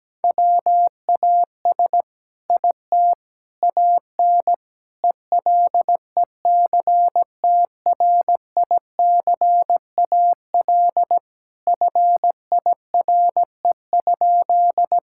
Morse code, heard as a keyed tone, about 17 wpm, 700 Hz. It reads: WAS IT AN ELECTRICAL FIRE?
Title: Sentences from Top 1000 Words - Review Part 6 - 17wpm